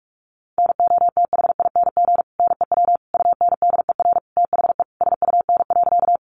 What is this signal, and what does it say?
Morse code, heard as a keyed tone, about 44 wpm, 700 Hz. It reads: NOTHING NEW UNDER THE SUN.
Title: Sentences from Top 500 Words 45wpm